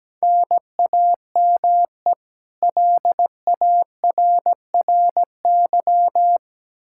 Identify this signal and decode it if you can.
Morse code, heard as a keyed tone, about 17 wpm, 700 Hz. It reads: NAME LARRY
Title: Practice QSO - Exchange 1 - Beginning - 17wpm